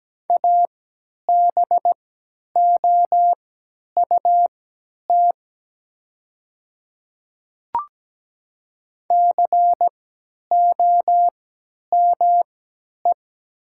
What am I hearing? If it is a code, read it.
Morse code, heard as a keyed tone, about 17 wpm, spaced as for 10 wpm, 700 Hz. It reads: ABOUT COME
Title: 3x Character Spacing - Top 300 Words - 17wpm